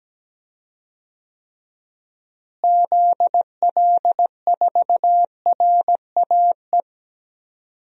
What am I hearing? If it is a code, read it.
Morse code, heard as a keyed tone, about 17 wpm, 700 Hz. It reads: ZL4RAE
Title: DXCC-100 Call Signs - 6 Characters - Repeated 3x - 17wpm